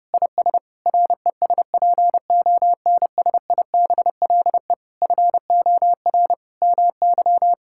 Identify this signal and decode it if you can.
Morse code, heard as a keyed tone, about 30 wpm, 700 Hz. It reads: IS RESPONSIBLE FOR MY
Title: Sentences from Top 1600 Words 30wpm